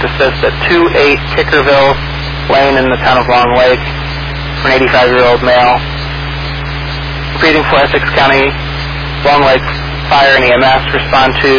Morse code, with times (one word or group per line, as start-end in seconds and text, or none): none